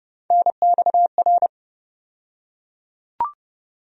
Morse code, heard as a keyed tone, about 30 wpm, 700 Hz. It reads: NXR
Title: USA Call Sign Postfixes - 3 Letters 30wpm